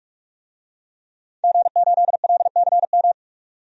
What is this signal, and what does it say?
Morse code, heard as a keyed tone, about 46 wpm, 700 Hz. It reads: G8LCM
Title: DXCC-100 Call Signs - 5 Characters - 45wpm